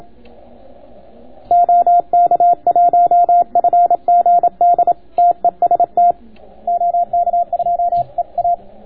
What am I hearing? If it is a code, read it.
Morse code, heard as a keyed tone, about 27 wpm, 700 Hz. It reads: OK1FGD TEST OK2EA